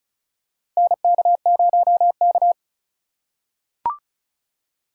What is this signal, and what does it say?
Morse code, heard as a keyed tone, about 36 wpm, 700 Hz. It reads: NK0K
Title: USA Call Signs - 4 Characters - 8x Character Spacing - 35wpm